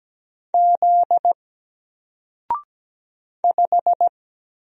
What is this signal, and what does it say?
Morse code, heard as a keyed tone, about 17 wpm, 700 Hz. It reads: Z 5
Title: Single Character (Letter:Number:Punctuation:Prosign) 17wpm